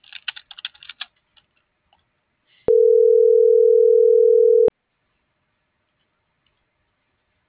Ringback tone (North America), 440 + 480 Hz.